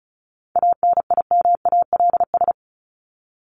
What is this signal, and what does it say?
Morse code, heard as a keyed tone, about 34 wpm, 700 Hz. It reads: ANIMALS
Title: ES:S Suffix - Top 100 Words with Base Words 35wpm